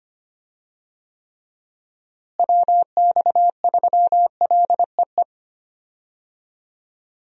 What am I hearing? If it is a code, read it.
Morse code, heard as a keyed tone, about 25 wpm, 700 Hz. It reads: WX3LEE
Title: USA Call Signs - 6 Characters - Repeated 2x - 25wpm